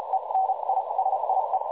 Morse code, heard as a keyed tone, about 36 wpm, spaced as for 29 wpm, 760 Hz, with noise. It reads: ENISIE
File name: dlazaawansowanych / nagraniazpasma / websdr recording 3565.0kHz